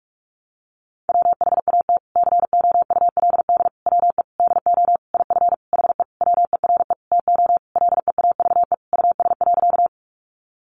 Morse code, heard as a keyed tone, about 44 wpm, 700 Hz. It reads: WHAT COULD WE DO IF HE WERE TO LEAVE US.
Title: Sentences from Top 400 Words 45wpm